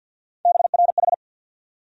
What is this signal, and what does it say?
Morse code, heard as a keyed tone, about 52 wpm, 700 Hz. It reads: BRH
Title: USA Call Sign Postfixes - 3 Letters 50wpm